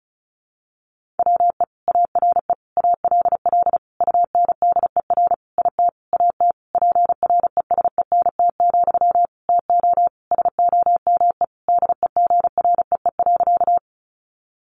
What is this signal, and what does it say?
Morse code, heard as a keyed tone, about 34 wpm, 700 Hz. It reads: WE ARE ALL UNDER IT AT PRESENT, TO SOME DEGREE.
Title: Sentences from Top 1500 Words 35wpm